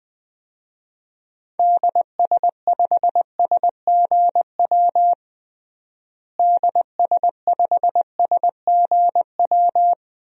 Morse code, heard as a keyed tone, about 20 wpm, 700 Hz. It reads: DS5SGW DS5SGW
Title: DXCC-100 Call Signs - 6 Characters - Repeated 3x - 20wpm